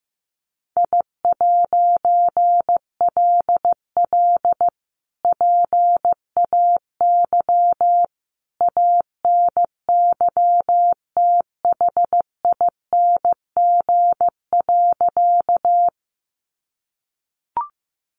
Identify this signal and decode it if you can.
Morse code, heard as a keyed tone, about 15 wpm, 700 Hz. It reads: I'LL PAY ANYTHING.
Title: A Sound Of Thunder 15wpm